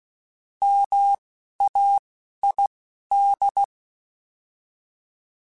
Morse code, heard as a keyed tone, about 16 wpm, spaced as for 11 wpm, 800 Hz. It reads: MAID
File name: Alphabetical / Alphabetical mp3s / Ms @10